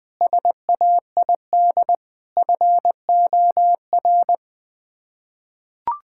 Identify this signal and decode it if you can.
Morse code, heard as a keyed tone, about 21 wpm, 700 Hz. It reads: SAID FOR
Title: Sets of 6 Words - 20wpm